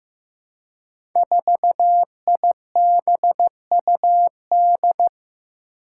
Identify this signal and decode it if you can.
Morse code, heard as a keyed tone, about 15 wpm, 700 Hz. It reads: 4IBUD
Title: Five Challenging Letter-Number Combinations 15wpm